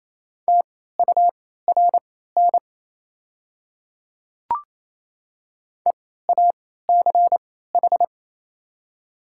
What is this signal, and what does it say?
Morse code, heard as a keyed tone, about 28 wpm, spaced as for 16 wpm, 700 Hz. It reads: TURN EACH
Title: 3x Character Spacing - Top 200 Words - 28wpm